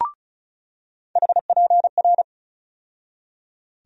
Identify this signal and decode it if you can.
Morse code, heard as a keyed tone, about 34 wpm, 700 Hz. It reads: HPR
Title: USA Call Sign Postfixes - 3 Letters 35wpm